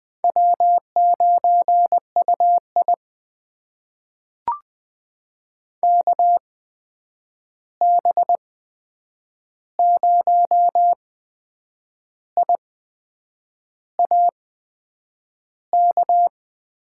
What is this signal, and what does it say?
Morse code, heard as a keyed tone, about 20 wpm, spaced as for 7 wpm, 700 Hz. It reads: W9UI KB0IAK
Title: USA Call Signs - 8x Character Spacing - 20wpm